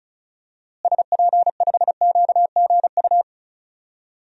Morse code, heard as a keyed tone, about 36 wpm, 700 Hz. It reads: SP5QGU
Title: DXCC-100 Call Signs - 6 Characters - Repeated 3x - 35wpm